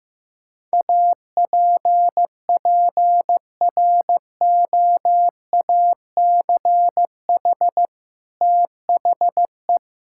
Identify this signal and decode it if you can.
Morse code, heard as a keyed tone, about 15 wpm, 700 Hz. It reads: APPROACH THE
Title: Sentences from Top 1000 Words - Review Part 6 - 15wpm